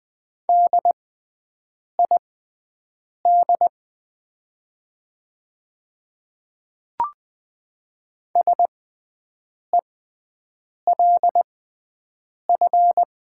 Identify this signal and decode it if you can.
Morse code, heard as a keyed tone, about 20 wpm, spaced as for 7 wpm, 700 Hz. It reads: DID SELF
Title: 6x Character Spacing - Top 200 Words - 20wpm